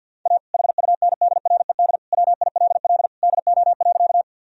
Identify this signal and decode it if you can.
Morse code, heard as a keyed tone, about 52 wpm, 700 Hz. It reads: A HUNDRED WILL DO.
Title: Sentences from Top 1000 Words - Review Part 2 - 50wpm